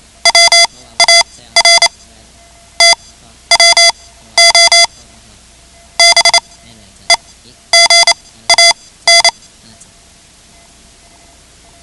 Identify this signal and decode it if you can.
Morse code, heard as a keyed tone, about 28 wpm, spaced as for 14 wpm, 2050 Hz. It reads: WAR TWO BEGAN
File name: ham radio / CodeQuick / Code Quick 10